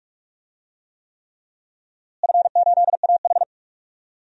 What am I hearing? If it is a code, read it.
Morse code, heard as a keyed tone, about 46 wpm, 700 Hz. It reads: F8AH
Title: DXCC-100 Call Signs - 4 Characters - Repeated 3x - 45wpm